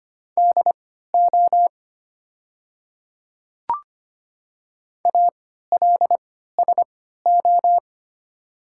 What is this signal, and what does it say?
Morse code, heard as a keyed tone, about 25 wpm, spaced as for 14 wpm, 700 Hz. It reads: DO ALSO
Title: 3x Character Spacing - Top 200 Words - 25wpm